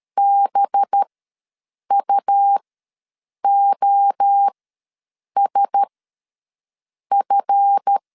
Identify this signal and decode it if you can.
Morse code, heard as a keyed tone, about 13 wpm, spaced as for 7 wpm, 800 Hz. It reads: BUOSF
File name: 7 GF9Z ws 12wpm